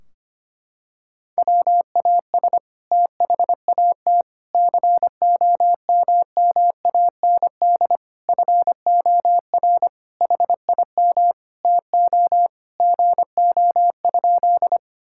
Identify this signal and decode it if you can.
Morse code, heard as a keyed tone, about 25 wpm, 700 Hz. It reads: WAS THAT COMMAND FOR HIM TO GO?
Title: Sentences from Top 1600 Words 25wpm